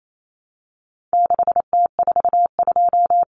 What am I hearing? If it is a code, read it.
Morse code, heard as a keyed tone, about 28 wpm, 700 Hz. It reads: 6T42
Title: Common Cut Numbers - 4-Digit 28wpm